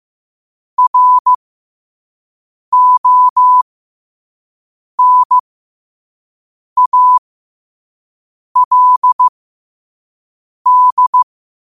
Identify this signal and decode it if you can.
Morse code, heard as a keyed tone, about 15 wpm, spaced as for 5 wpm, 1000 Hz. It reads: RONALD
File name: qotdfiles / 5wpm